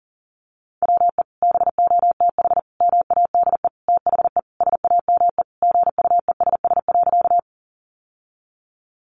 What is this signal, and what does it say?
Morse code, heard as a keyed tone, about 40 wpm, 700 Hz. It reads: WE BOTH MADE THE SAME GUESS.
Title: Sentences from Top 1500 Words 40wpm